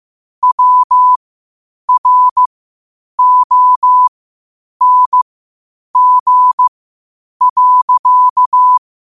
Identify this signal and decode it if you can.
Morse code, heard as a keyed tone, about 15 wpm, spaced as for 8 wpm, 1000 Hz. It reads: WRONG.
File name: qotdfiles / 8wpm